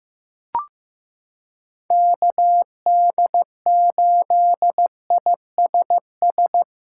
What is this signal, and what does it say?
Morse code, heard as a keyed tone, about 15 wpm, 700 Hz. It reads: KD8ISS
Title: USA Callsign DE Callsign - 6 Characters - 15wpm